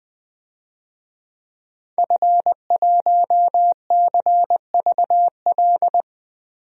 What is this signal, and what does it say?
Morse code, heard as a keyed tone, about 20 wpm, 700 Hz. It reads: F1CVL
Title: DXCC-100 Call Signs - 5 Characters - 20wpm